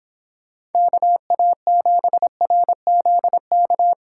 Morse code, heard as a keyed tone, about 29 wpm, 700 Hz. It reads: KA7RZK